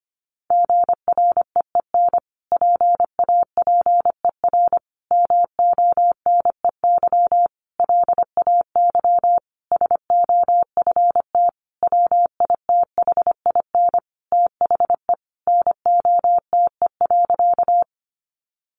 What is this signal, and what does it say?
Morse code, heard as a keyed tone, about 25 wpm, 700 Hz. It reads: GREEN PAPER MONEY LAY SOFT WITHIN THE NOTE.
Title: Sentences from Top 1500 Words 25wpm